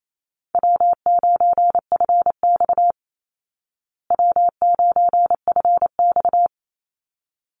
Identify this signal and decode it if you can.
Morse code, heard as a keyed tone, about 28 wpm, 700 Hz. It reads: W9FX W9FX
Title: Easy Call Signs Repeated 3x - 28wpm